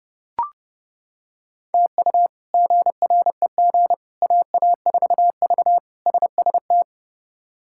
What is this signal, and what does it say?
Morse code, heard as a keyed tone, about 30 wpm, 700 Hz. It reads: TU GREG AA4V SST
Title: K1USN SST QSO Practice - 30wpm